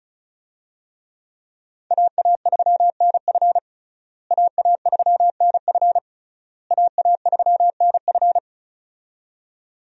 Morse code, heard as a keyed tone, about 36 wpm, 700 Hz. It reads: AA3NF AA3NF AA3NF
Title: USA Call Signs - 5 Characters - Repeated 3x - 35wpm